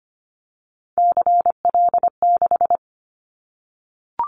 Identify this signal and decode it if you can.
Morse code, heard as a keyed tone, about 25 wpm, 700 Hz. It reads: CL6
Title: Callsign Prefixes for Top 100 DXCC - 25wpm